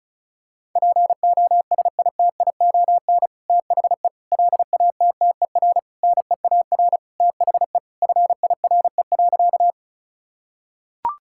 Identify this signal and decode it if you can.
Morse code, heard as a keyed tone, about 34 wpm, 700 Hz. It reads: POSITION THE LATTER NEAR THE FIRE.
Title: Sentences from Top 1600 Words 35wpm